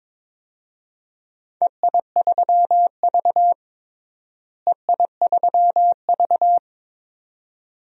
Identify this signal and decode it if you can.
Morse code, heard as a keyed tone, about 22 wpm, 700 Hz. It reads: EI3V EI3V
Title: DXCC-100 Call Signs - 4 Characters - Repeated 2x - 22wpm